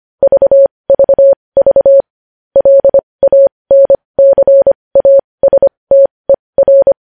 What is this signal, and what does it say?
Morse code, heard as a keyed tone, about 25 wpm, spaced as for 20 wpm, 550 Hz. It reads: VVV LANCASTER
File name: podcast / News headlines-1x-25x20